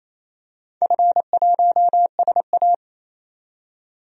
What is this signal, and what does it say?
Morse code, heard as a keyed tone, about 28 wpm, 700 Hz. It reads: F1SA
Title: DXCC-100 Call Signs - 4 Characters - 28wpm